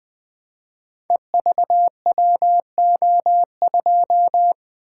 Morse code, heard as a keyed tone, about 21 wpm, 700 Hz. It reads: EVWO2
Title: Code Groups - 5 Letter:Numbers - 20wpm